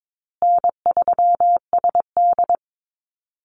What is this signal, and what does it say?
Morse code, heard as a keyed tone, about 22 wpm, 700 Hz. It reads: N3SD
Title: Easy Call Signs Repeated 2x - 22wpm